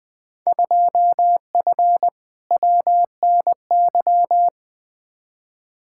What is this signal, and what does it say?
Morse code, heard as a keyed tone, about 20 wpm, 700 Hz. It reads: 2F WNY
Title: Field Day Exchanges Encore - 20wpm